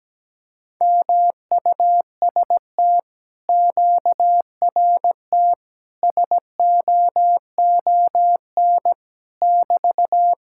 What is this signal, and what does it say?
Morse code, heard as a keyed tone, about 17 wpm, 700 Hz. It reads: MUST QRT SOON <BT>
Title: Speed-Racing Common QSO Phrases - 17wpm